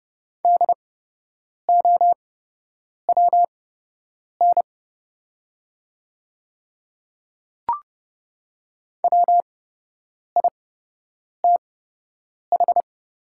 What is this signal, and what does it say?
Morse code, heard as a keyed tone, about 30 wpm, spaced as for 8 wpm, 700 Hz. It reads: DOWN WITH
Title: 8x Character Spacing - Top 100 Words - 30wpm